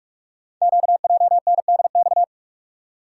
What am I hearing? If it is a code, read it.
Morse code, heard as a keyed tone, about 44 wpm, 700 Hz. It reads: QJNDX
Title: Code Groups - 5 Letters - 45wpm